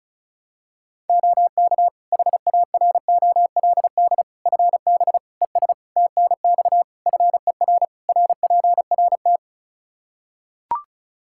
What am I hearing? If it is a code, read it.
Morse code, heard as a keyed tone, about 36 wpm, 700 Hz. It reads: OK HAROLD FB ES TNX FER RPRT
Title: Practice QSO - Exchange 2 - Info - 35wpm